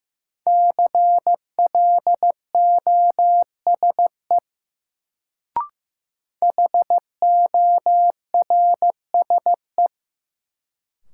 Morse code, heard as a keyed tone, about 15 wpm, 700 Hz. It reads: CLOSE HORSE
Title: Sentences from Top 300 Words - New Word Drill - 15wpm